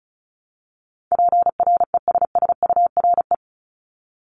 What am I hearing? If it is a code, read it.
Morse code, heard as a keyed tone, about 36 wpm, 700 Hz. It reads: PRESSURE
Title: Multi-letter Phonemes Repeated 3x 35wpm